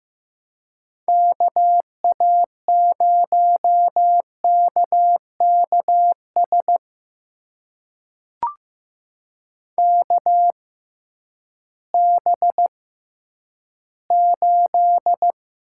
Morse code, heard as a keyed tone, about 15 wpm, spaced as for 10 wpm, 700 Hz. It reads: KA0KKS KB8
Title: USA Call Signs - 6 Characters - 6x Character Spacing - 15wpm